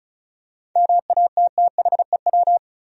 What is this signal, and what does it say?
Morse code, heard as a keyed tone, about 34 wpm, 700 Hz. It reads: MATTHEW
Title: USA Common Names - 35wpm